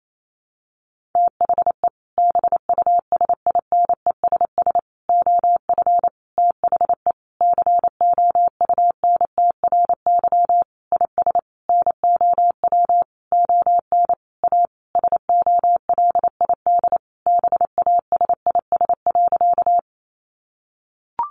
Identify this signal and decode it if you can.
Morse code, heard as a keyed tone, about 28 wpm, 700 Hz. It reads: THE BUSINESS OF THE COUNTRY IS NOW ON A SOLID BASIS.
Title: Sentences from Top 1500 Words 28wpm